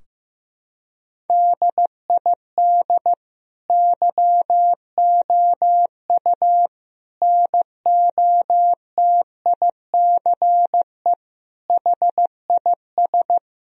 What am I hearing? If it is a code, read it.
Morse code, heard as a keyed tone, about 16 wpm, 700 Hz. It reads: DID YOU NOTICE HIS
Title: Sentences from Top 1000 Words - Review Part 3 - 15wpm